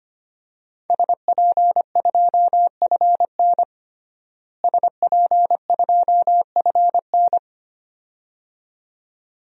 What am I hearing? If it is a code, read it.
Morse code, heard as a keyed tone, about 25 wpm, 700 Hz. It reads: SP2FN SP2FN
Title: DXCC-100 Call Signs - 5 Characters - Repeated 2x - 25wpm